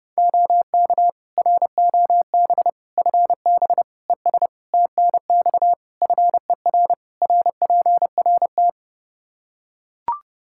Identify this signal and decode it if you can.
Morse code, heard as a keyed tone, about 32 wpm, 700 Hz. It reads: OK ROB FB ES TNX FER RPRT
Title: Practice QSO - Exchange 2 - Info - 30wpm